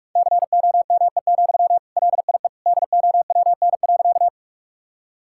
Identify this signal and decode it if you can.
Morse code, heard as a keyed tone, about 41 wpm, 700 Hz. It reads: COME, LIE DOWN.
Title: Sentences from Top 1600 Words 45wpm